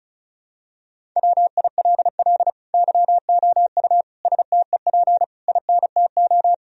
Morse code, heard as a keyed tone, about 36 wpm, 700 Hz. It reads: WILL YOU STEP INTO
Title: Sentences from Top 1000 Words - Review Part 2 - 35wpm